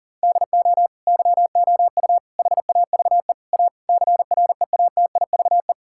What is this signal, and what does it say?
Morse code, heard as a keyed tone, about 40 wpm, 700 Hz. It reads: DO YOU HAVE A CREATIVE
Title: Sentences from Top 1000 Words - Review Part 5 - 40wpm